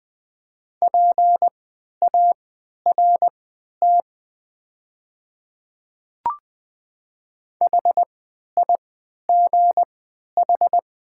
Morse code, heard as a keyed tone, about 20 wpm, spaced as for 11 wpm, 700 Hz. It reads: PART HIGH
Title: 3x Character Spacing - Top 300 Words - 20wpm